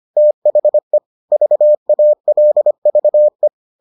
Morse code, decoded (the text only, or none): THE VALVE